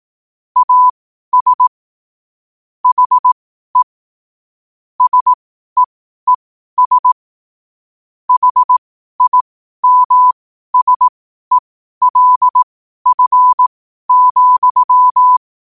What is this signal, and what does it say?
Morse code, heard as a keyed tone, about 18 wpm, spaced as for 12 wpm, 1000 Hz. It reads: AS HE SEES HIMSELF,
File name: qotdfiles / 12wpm